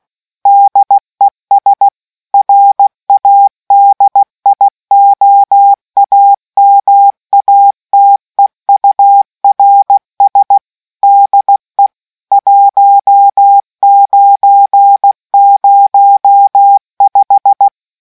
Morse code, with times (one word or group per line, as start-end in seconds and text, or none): 0.5-1.9 DES
2.3-10.6 RADIOAMATEURS
11.0-11.9 DE
12.3-17.7 1905